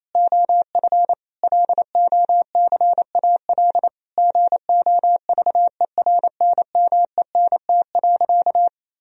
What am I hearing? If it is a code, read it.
Morse code, heard as a keyed tone, about 28 wpm, 700 Hz. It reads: OF LOCAL GOVERNMENT.
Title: Sentences from Top 1000 Words - Review Part 3 - 28wpm